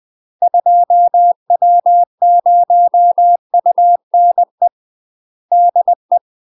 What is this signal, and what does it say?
Morse code, heard as a keyed tone, about 20 wpm, 700 Hz. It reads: 2W0UNE DE
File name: UK MASTER QSOs TRAINING / 20WPM / UK MASTER QSOs 0052